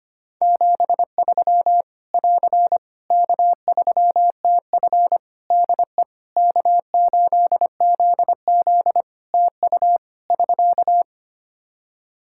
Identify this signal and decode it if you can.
Morse code, heard as a keyed tone, about 25 wpm, 700 Hz. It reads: 73 <AR> K3TF DE K8ZZ TU <SK>
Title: K1USN SST QSO Practice 25wpm